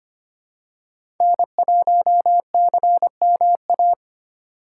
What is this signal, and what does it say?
Morse code, heard as a keyed tone, about 25 wpm, 700 Hz. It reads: N1CMA
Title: USA Call Signs - 5 Characters - 6x Character Spacing - 25wpm